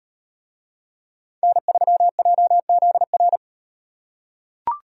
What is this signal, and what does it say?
Morse code, heard as a keyed tone, about 36 wpm, 700 Hz. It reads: N3JZR